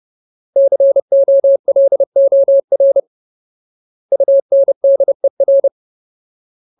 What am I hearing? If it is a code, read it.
Morse code, heard as a keyed tone, about 30 wpm, 550 Hz. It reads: COLOR UNDER
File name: Ultimate Word List / 30WPM Ultimate